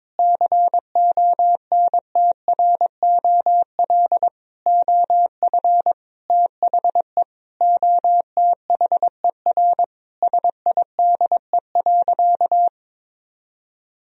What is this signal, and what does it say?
Morse code, heard as a keyed tone, about 22 wpm, 700 Hz. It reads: CONTROL OF THE OTHER SIDE.
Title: Sentences from Top 1000 Words - Review Part 3 - 22wpm